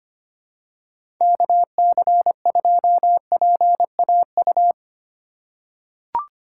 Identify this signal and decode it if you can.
Morse code, heard as a keyed tone, about 25 wpm, 700 Hz. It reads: KC2PAU
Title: USA Call Signs - 6 Characters - 6x Character Spacing - 25wpm